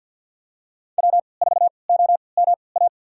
Morse code, heard as a keyed tone, about 53 wpm, spaced as for 29 wpm, 690 Hz. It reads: W4OMA
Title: USA Call Signs - 5 Characters - 3x Character Spacing - 50wpm